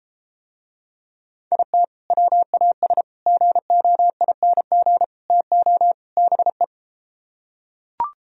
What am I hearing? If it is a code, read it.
Morse code, heard as a keyed tone, about 33 wpm, 700 Hz. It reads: IT WAS GOING TO BE